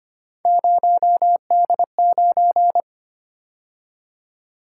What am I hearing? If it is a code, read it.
Morse code, heard as a keyed tone, about 29 wpm, 700 Hz. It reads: 0D9
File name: Speed-Racing - Three Letter-Number Combinations - 25wpm